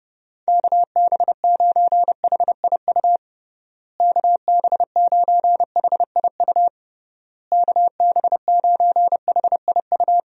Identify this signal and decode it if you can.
Morse code, heard as a keyed tone, about 30 wpm, 700 Hz. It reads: KB9HIU KB9HIU KB9HIU
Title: USA Call Signs - 6 Characters - Repeated 3x - 30wpm